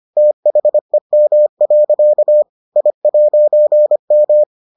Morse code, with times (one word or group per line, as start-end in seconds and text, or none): 0.2-2.4 THEM.
2.8-4.4 I'M